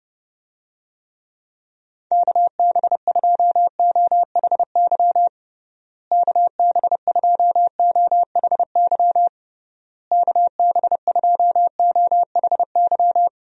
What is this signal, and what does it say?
Morse code, heard as a keyed tone, about 32 wpm, 700 Hz. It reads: KB2OHY KB2OHY KB2OHY
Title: USA Call Signs - 6 Characters - Repeated 3x - 30wpm